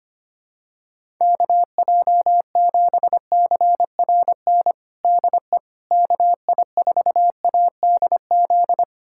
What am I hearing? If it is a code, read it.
Morse code, heard as a keyed tone, about 25 wpm, 700 Hz. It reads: KJ7CRN DE KI4ADZ
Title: USA Callsign DE Callsign - 6 Characters - 25wpm